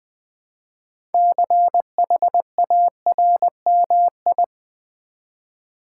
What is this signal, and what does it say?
Morse code, heard as a keyed tone, about 20 wpm, 700 Hz. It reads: CHARMI
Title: Build A Word - 3 Letters At A Time - 8-Letter Words - 20wpm